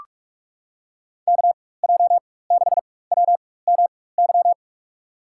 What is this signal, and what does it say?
Morse code, heard as a keyed tone, about 42 wpm, spaced as for 20 wpm, 700 Hz. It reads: KJ6WMY
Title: USA Call Signs - 6 Characters - 4x Character Spacing - 45wpm